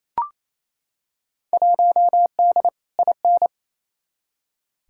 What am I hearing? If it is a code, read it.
Morse code, heard as a keyed tone, about 27 wpm, 700 Hz. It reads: 1D IN
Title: Field Day Exchanges Encore - 28wpm